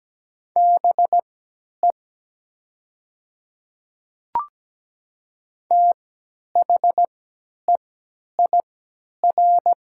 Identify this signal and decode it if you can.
Morse code, heard as a keyed tone, about 17 wpm, spaced as for 10 wpm, 700 Hz. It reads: BE THEIR
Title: 3x Character Spacing - Top 100 Words - 17wpm